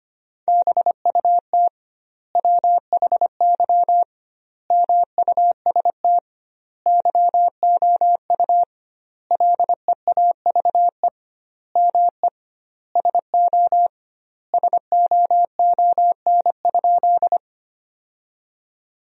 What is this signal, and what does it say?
Morse code, heard as a keyed tone, about 25 wpm, 700 Hz. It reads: BUT WHY MUST YOU LEAVE ME SO SOON?
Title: 2x Word-Spacing - Sentences from Top 400 Words - 25wpm